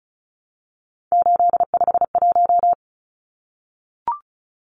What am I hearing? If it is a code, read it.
Morse code, heard as a keyed tone, about 34 wpm, 700 Hz. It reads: 851
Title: Numbers - 3-Digit 35wpm